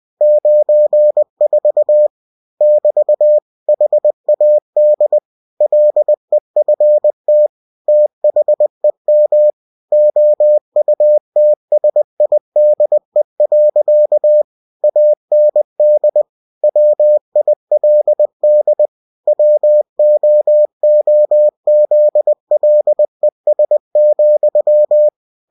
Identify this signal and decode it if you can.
Morse code, heard as a keyed tone, about 20 wpm, 600 Hz. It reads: 94 <BT> HAD LEFT THEM OUTSIDE. AND WILD WOOZLES,
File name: cw-books / winnie-the-pooh / winnie-20wpm-page-0094